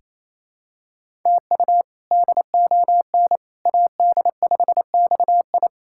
Morse code, heard as a keyed tone, about 28 wpm, 700 Hz. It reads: TU DON AD5XI
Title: K1USN SST QSO Practice - 28wpm